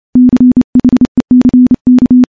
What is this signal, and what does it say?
Morse code, heard as a keyed tone, about 29 wpm, 250 Hz. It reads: CHECK
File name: morse / morse-training-30WPM